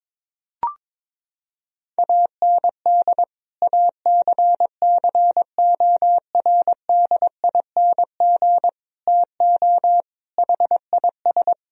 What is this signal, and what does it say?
Morse code, heard as a keyed tone, about 22 wpm, 700 Hz. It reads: AND ACCORDING TO HIS
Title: Sentences from Top 1000 Words - Review Part 4 - 22wpm